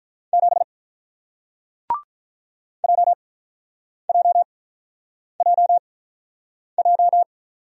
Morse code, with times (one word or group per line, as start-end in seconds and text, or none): 0.3-0.6 7
2.8-7.2 JJJJ